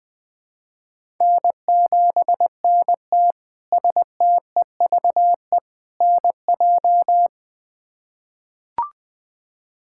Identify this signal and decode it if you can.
Morse code, heard as a keyed tone, about 20 wpm, 700 Hz. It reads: N7NT STEVE NJ
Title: K1USN SST QSO Practice - 20wpm